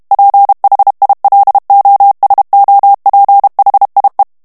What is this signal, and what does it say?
Morse code, heard as a keyed tone, about 32 wpm, 800 Hz. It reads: PHILOSOPHIE
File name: FRANCAIS TEXTE CLAIR 2 VITESSE 30 MOTS MINUTES